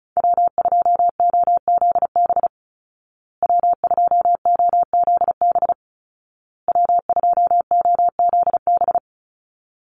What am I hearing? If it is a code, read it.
Morse code, heard as a keyed tone, about 34 wpm, 700 Hz. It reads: W2OZB W2OZB W2OZB
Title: Call Signs Repeated 2x - 35wpm